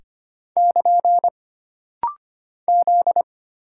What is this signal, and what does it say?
Morse code, heard as a keyed tone, about 25 wpm, 700 Hz. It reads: <KN> Z
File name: Single Character (Letter:Number:Punctuation:Prosign) - Rapid-Fire - 25wpm